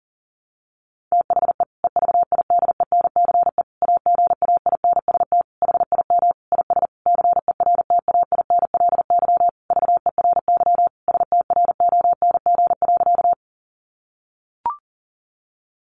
Morse code, heard as a keyed tone, about 40 wpm, 700 Hz. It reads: THE EVIDENCE AGAINST HIM IS CERTAINLY VERY STRONG.
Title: Sentences from Top 1300 Words 40wpm